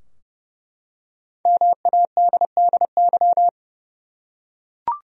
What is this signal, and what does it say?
Morse code, heard as a keyed tone, about 30 wpm, 700 Hz. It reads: MADDY
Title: USA Common Names - Max 5 Letters - 30wpm